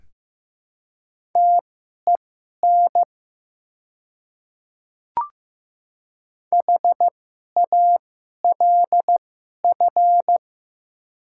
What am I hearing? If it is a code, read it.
Morse code, heard as a keyed tone, about 15 wpm, spaced as for 11 wpm, 700 Hz. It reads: TEN HALF
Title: 2x Character Spacing - Top 500 Words - 15wpm